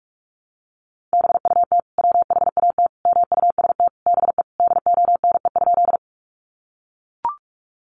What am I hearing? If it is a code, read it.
Morse code, heard as a keyed tone, about 44 wpm, 700 Hz. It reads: BUT WHAT MUST BE DONE?
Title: Sentences from Top 500 Words 45wpm